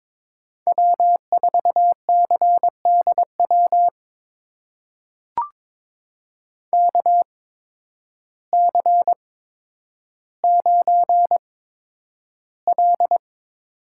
Morse code, heard as a keyed tone, about 22 wpm, spaced as for 10 wpm, 700 Hz. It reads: W4CDW KC9L
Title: USA Call Signs - 5 Characters - 8x Character Spacing - 22wpm